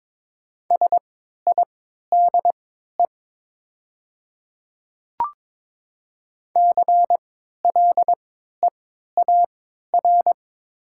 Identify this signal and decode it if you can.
Morse code, heard as a keyed tone, about 21 wpm, spaced as for 12 wpm, 700 Hz. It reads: SIDE CLEAR
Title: 3x Character Spacing - Top 500 Words - 22wpm